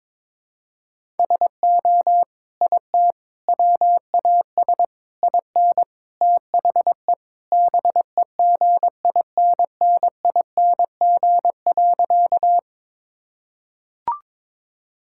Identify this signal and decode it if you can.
Morse code, heard as a keyed tone, about 21 wpm, 700 Hz. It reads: SO IT WAS IN THE BEGINNING.
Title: Sentences from Top 1000 Words - Review Part 5 - 22wpm